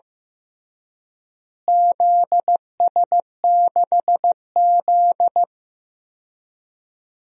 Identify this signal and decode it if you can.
Morse code, heard as a keyed tone, about 15 wpm, 700 Hz. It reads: ZS6Z
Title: DXCC-100 Call Signs - 4 Characters - Repeated 2x - 15wpm